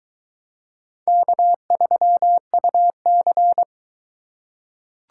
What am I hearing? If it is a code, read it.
Morse code, heard as a keyed tone, about 20 wpm, 700 Hz. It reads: K3UC